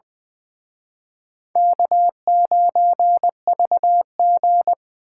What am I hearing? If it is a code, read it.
Morse code, heard as a keyed tone, about 24 wpm, 700 Hz. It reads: K9VG
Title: Speed Racing - USA 1x2 Call Signs 20wpm